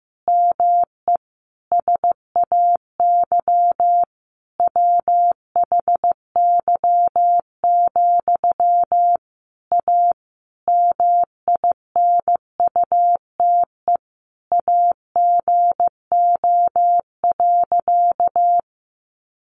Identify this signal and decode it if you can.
Morse code, heard as a keyed tone, about 15 wpm, 700 Hz. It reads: ME SAY WHY, A MINUTE AGO.
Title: Sentences from Top 500 Words 15wpm